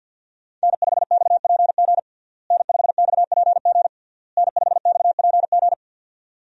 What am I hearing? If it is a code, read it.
Morse code, heard as a keyed tone, about 52 wpm, 700 Hz. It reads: N5XPG N5XPG N5XPG
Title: USA Call Signs - 5 Characters - Repeated 3x - 50wpm